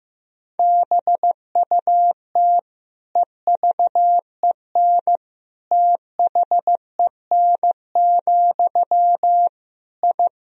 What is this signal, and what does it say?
Morse code, heard as a keyed tone, about 15 wpm, 700 Hz. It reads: BUT EVEN THEN, I
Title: Sentences from Top 1000 Words - Review Part 1 - 15wpm